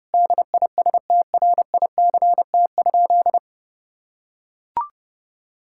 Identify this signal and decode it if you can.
Morse code, heard as a keyed tone, about 30 wpm, 700 Hz. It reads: DISTRICT?
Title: Sentences from Top 1600 Words 30wpm